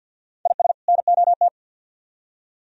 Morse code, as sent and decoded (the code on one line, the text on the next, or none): .. ... / -. --- -
IS NOT